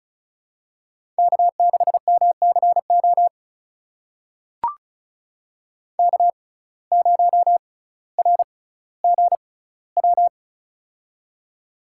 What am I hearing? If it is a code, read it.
Morse code, heard as a keyed tone, about 36 wpm, spaced as for 18 wpm, 700 Hz. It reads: K6MCO K0RGW
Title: USA Call Signs - 5 Characters - 6x Character Spacing - 35wpm